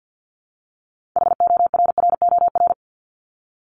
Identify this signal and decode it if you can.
Morse code, heard as a keyed tone, about 50 wpm, 700 Hz. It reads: HORROR
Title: Top 1501-1750 Words 50wpm